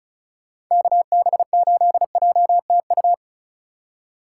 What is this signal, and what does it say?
Morse code, heard as a keyed tone, about 43 wpm, 700 Hz. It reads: KB8JTU
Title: Speed Racing - USA 2x3 Call Signs 35wpm